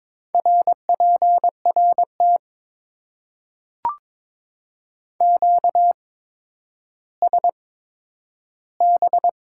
Morse code, heard as a keyed tone, about 22 wpm, spaced as for 11 wpm, 700 Hz. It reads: RPRT QSB
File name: Top 90 QSO Elements - 8x Character Spacing - 22wpm